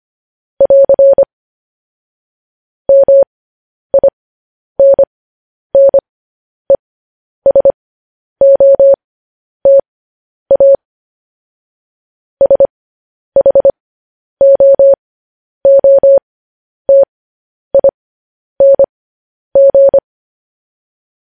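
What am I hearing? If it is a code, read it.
Morse code, heard as a keyed tone, about 25 wpm, spaced as for 10 wpm, 550 Hz. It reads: <AR> MINNESOTA SHOOTING
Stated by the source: Morse code news headlines.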